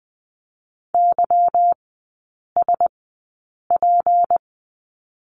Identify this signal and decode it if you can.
Morse code, heard as a keyed tone, about 20 wpm, spaced as for 8 wpm, 700 Hz. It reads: YSP
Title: 152 Lesson 25e - Letter p - Three Characters 20wpm